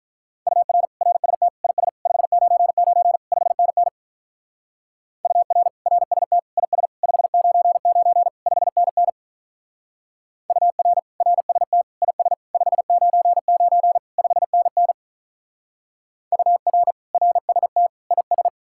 Morse code, heard as a keyed tone, about 42 wpm, 700 Hz. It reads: UR RST IS 599 5NN UR RST IS 599 5NN UR RST IS 599 5NN UR RST IS
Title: Speed-Racing Common QSO Phrases Encore - 35wpm